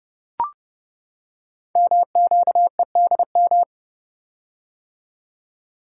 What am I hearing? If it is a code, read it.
Morse code, heard as a keyed tone, about 30 wpm, 700 Hz. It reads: MQEDM
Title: Code Groups - 5 Letter:Numbers - 30wpm